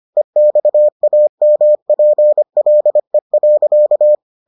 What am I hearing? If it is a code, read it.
Morse code, heard as a keyed tone, about 25 wpm, 600 Hz. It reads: EXAMPLE.